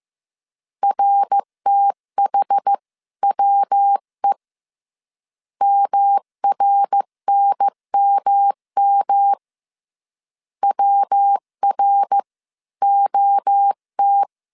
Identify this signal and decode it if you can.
Morse code, heard as a keyed tone, about 15 wpm, 800 Hz. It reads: RTHWE MRNMM WR OT
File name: CW / DET / 1 3 1500C S C 14wpm 2458